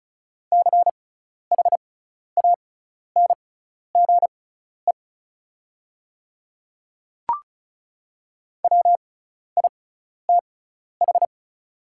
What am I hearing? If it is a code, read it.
Morse code, heard as a keyed tone, about 36 wpm, spaced as for 12 wpm, 690 Hz. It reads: CHANGE WITH
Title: 6x Character Spacing - Top 200 Words - 35wpm